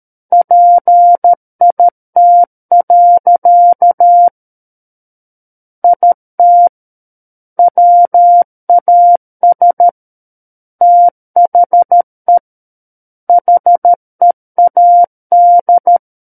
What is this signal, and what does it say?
Morse code, heard as a keyed tone, about 13 wpm, 700 Hz. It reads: PIT. IT WAS THE HEAD
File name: Book One / 04-Chapter 4 - 13 WPM